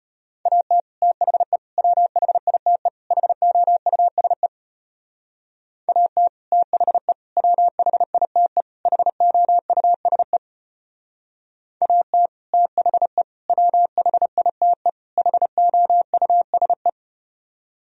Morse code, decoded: AT THE WHITE HOUSE AT THE WHITE HOUSE AT THE WHITE HOUSE

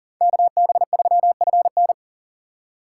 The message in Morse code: KB3FN

